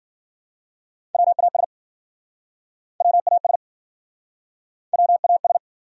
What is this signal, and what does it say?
Morse code, heard as a keyed tone, about 52 wpm, 700 Hz. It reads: WAS WAS WAS